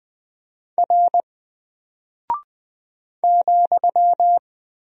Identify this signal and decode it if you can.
Morse code, heard as a keyed tone, about 20 wpm, 700 Hz. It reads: R ,
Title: Single Character (Letter:Number:Punctuation:Prosign) Alt 20wpm